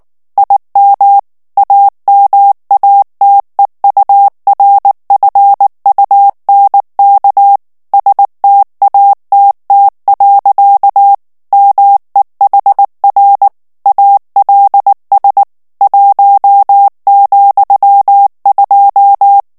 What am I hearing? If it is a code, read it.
Morse code, heard as a keyed tone, about 19 wpm, 800 Hz. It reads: IM AMATEURFUNK STATT. MEHR ALS 1,2
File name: morse mp3 / ALLEMAND texte clair 2 vitesse 18 mots minute